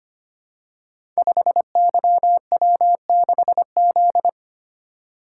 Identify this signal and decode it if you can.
Morse code, heard as a keyed tone, about 25 wpm, 700 Hz. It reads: 5YW6Z
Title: Code Groups - 5 Letter:Numbers - 25wpm